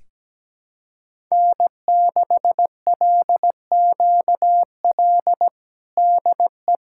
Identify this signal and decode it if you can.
Morse code, heard as a keyed tone, about 17 wpm, 700 Hz. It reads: N6LQL DE
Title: USA Callsign DE Callsign - 5 Characters - 17wpm